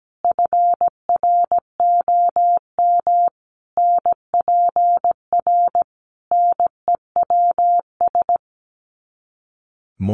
Morse code, heard as a keyed tone, about 17 wpm, 700 Hz. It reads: FROM NPR NEWS
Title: Sets of 5 Words Encore 17wpm